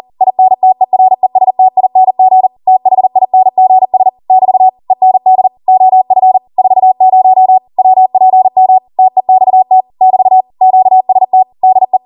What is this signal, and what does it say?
Morse code, heard as a keyed tone, about 40 wpm, 750 Hz. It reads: INTERESTING THINGS <BT> END OF 40 WPM TEXT <BT> QST DE